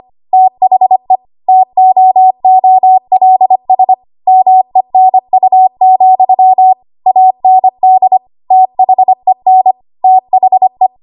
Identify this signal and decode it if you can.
Morse code, heard as a keyed tone, about 25 wpm, 750 Hz. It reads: THE TOOLS MENU, AND THEN THE
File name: Archive / 25 WPM / 25WPM